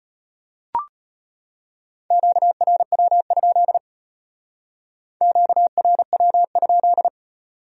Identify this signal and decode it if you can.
Morse code, heard as a keyed tone, about 31 wpm, 700 Hz. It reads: QRW? QRW?
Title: Speed-Racing Q-Codes - 25wpm